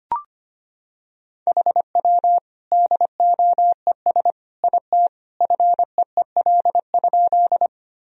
Morse code, heard as a keyed tone, about 25 wpm, 700 Hz. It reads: HW DOES IT FEEL?
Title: Sentences from Top 400 Words and Abbreviations 25wpm